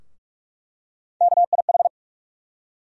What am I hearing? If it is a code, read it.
Morse code, heard as a keyed tone, about 44 wpm, 700 Hz. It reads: KIH
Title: USA Call Sign Postfixes - 3 Letters 45wpm